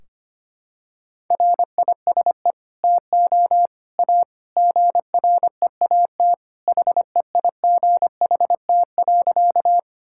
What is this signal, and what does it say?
Morse code, heard as a keyed tone, about 25 wpm, 700 Hz. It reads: RISE TO A GREAT HEIGHT.
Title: Sentences from Top 1600 Words 25wpm